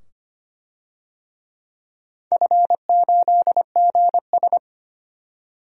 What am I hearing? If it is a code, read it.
Morse code, heard as a keyed tone, about 25 wpm, 700 Hz. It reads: F8GS
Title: DXCC-100 Call Signs - 4 Characters - 25wpm